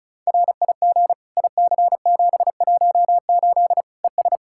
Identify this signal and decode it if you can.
Morse code, heard as a keyed tone, about 36 wpm, 700 Hz. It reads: RIG IC718 ES
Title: Practice QSO - Exchange 2 - Info - 35wpm